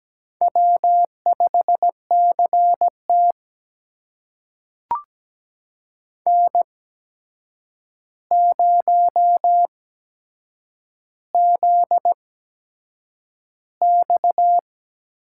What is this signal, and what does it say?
Morse code, heard as a keyed tone, about 17 wpm, spaced as for 7 wpm, 700 Hz. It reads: W5CT N0ZX
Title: USA Call Signs - 4 Characters - 8x Character Spacing - 17wpm